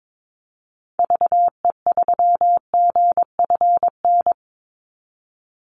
Morse code, heard as a keyed tone, about 22 wpm, 700 Hz. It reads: VE3GFN
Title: Call Signs - 22wpm